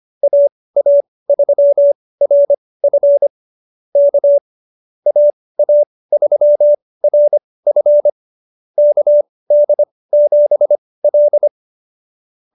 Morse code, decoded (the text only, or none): AA3RF K AA3RF KD7L